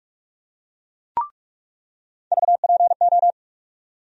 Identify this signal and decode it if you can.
Morse code, heard as a keyed tone, about 46 wpm, 700 Hz. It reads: VPO
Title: USA Call Sign Postfixes - 3 Letters 45wpm